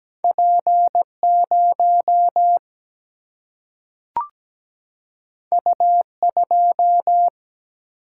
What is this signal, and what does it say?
Morse code, heard as a keyed tone, about 17 wpm, 700 Hz. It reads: P0 U2
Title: Single-Letter Single-Number 17wpm